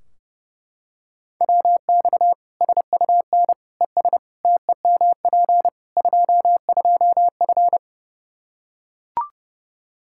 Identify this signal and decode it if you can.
Morse code, heard as a keyed tone, about 32 wpm, 700 Hz. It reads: WX SUN ES TEMP 22F
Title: Practice QSO - Exchange 2 - Info - 30wpm